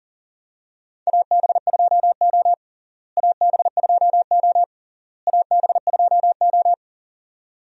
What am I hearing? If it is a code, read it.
Morse code, heard as a keyed tone, about 40 wpm, 700 Hz. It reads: AB2O AB2O AB2O
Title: USA Call Signs - 4 Characters - Repeated 3x - 40wpm